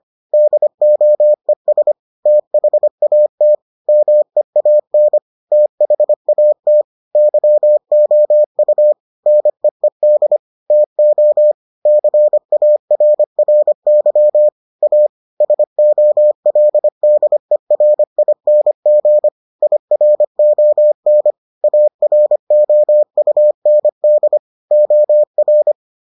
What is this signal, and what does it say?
Morse code, heard as a keyed tone, about 25 wpm, 600 Hz. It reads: DOES THAT MEAN THAT YOU NEED TO CARRY A SOLDERING IRON AROUND OR